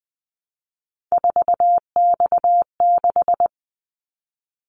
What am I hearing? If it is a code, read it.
Morse code, heard as a keyed tone, about 20 wpm, 700 Hz. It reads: 4X6
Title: Callsign Prefixes for Top 50 DXCC - 20wpm